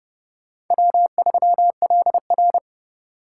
Morse code, heard as a keyed tone, about 29 wpm, 700 Hz. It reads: W3LR